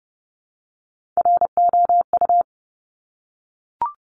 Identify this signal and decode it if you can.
Morse code, heard as a keyed tone, about 30 wpm, 700 Hz. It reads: ROU
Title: Top Three Letter Combinations in Words 30wpm